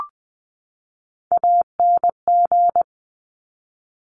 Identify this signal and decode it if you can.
Morse code, heard as a keyed tone, about 20 wpm, 700 Hz. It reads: ANG
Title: Top Three Letter Combinations in Words 20wpm